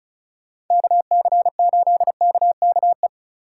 Morse code, heard as a keyed tone, about 45 wpm, 700 Hz. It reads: KC8KKE